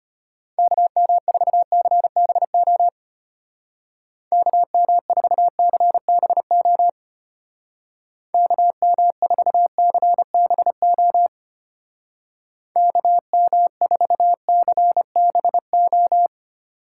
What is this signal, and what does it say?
Morse code, heard as a keyed tone, about 32 wpm, 700 Hz. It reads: KM4CBO KM4CBO KM4CBO KM4CBO